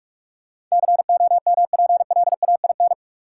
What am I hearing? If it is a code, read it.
Morse code, heard as a keyed tone, about 46 wpm, 700 Hz. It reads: COMPLAIN
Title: Sentences from Top 1300 Words - New Word Drill Alt - 45wpm